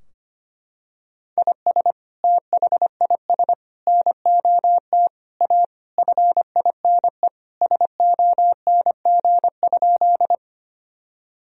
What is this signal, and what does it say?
Morse code, heard as a keyed tone, about 25 wpm, 700 Hz. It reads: IS THIS NOT A FINE SONG?
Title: Sentences from Top 1000 Words - Review Part 2 - 25wpm